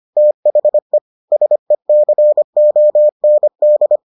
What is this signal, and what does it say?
Morse code, heard as a keyed tone, about 25 wpm, 600 Hz. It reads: THE SECOND